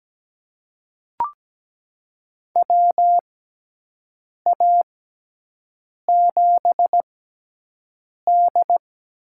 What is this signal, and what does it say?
Morse code, heard as a keyed tone, about 17 wpm, spaced as for 6 wpm, 700 Hz. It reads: WA7D
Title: USA Call Signs - 4 Characters - 6x Character Spacing - 17wpm